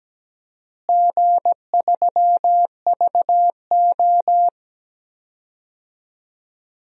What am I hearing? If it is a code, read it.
Morse code, heard as a keyed tone, about 17 wpm, 700 Hz. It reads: G3VO
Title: DXCC-100 Call Signs - 4 Characters - Repeated 2x - 17wpm